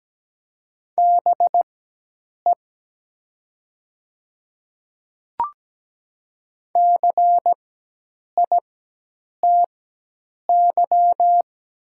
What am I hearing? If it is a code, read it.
Morse code, heard as a keyed tone, about 17 wpm, spaced as for 8 wpm, 700 Hz. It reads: BE CITY
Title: 4x Character Spacing - Top 300 Words - 17wpm